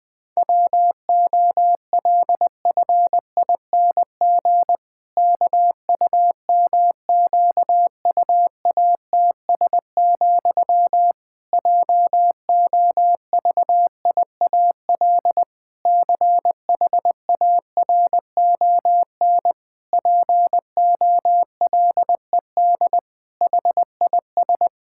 Morse code, heard as a keyed tone, about 20 wpm, 700 Hz. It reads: WOLFING KUMQUATS, JOVIAL CHARON POLED HIS
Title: Pangram Challenge v2 - 20wpm